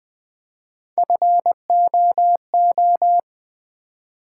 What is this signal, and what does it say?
Morse code, heard as a keyed tone, about 20 wpm, 700 Hz. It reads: FOO
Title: Build A Word - 1 Letter At A Time - 4-Letter Words - 20wpm